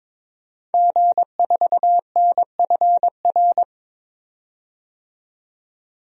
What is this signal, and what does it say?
Morse code, heard as a keyed tone, about 22 wpm, 700 Hz. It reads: G4NFR
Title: DXCC-100 Call Signs - 5 Characters - 22wpm